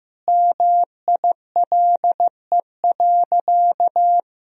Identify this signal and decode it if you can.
Morse code, heard as a keyed tone, about 15 wpm, 700 Hz. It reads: MILE.
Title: Sentences from Top 1000 Words - Review Part 4 - 15wpm